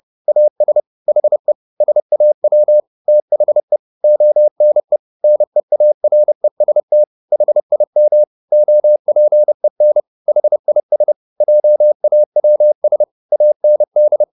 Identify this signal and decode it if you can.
Morse code, heard as a keyed tone, about 30 wpm, 600 Hz. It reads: AS HE SAW THE ONE NEAREST HIM OPEN HIS JAWS AND